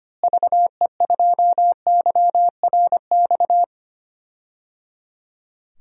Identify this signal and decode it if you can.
Morse code, heard as a keyed tone, about 25 wpm, 700 Hz. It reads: VE2YRX
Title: DXCC-100 Call Signs - 6 Characters - Repeated 3x - 25wpm